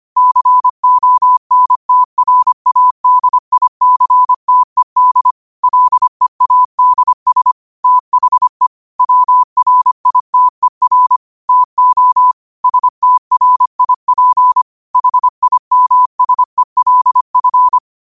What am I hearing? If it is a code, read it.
Morse code, heard as a keyed tone, about 25 wpm, 1000 Hz. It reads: CONTRADICTED LEADS THE WRITER TO STRIP HIMSELF